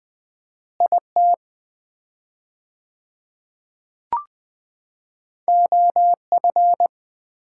Morse code, heard as a keyed tone, about 20 wpm, 700 Hz. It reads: IT OF
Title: 4x Word-Spacing - Sets of 3 Words - 20wpm